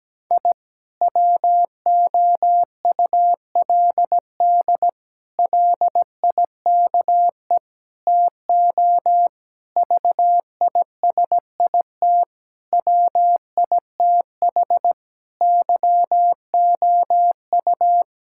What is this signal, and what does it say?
Morse code, heard as a keyed tone, about 17 wpm, 700 Hz. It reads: I WOULD LIKE TO VISIT WITH YOU
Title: Sentences from Top 1000 Words - Review Part 6 - 17wpm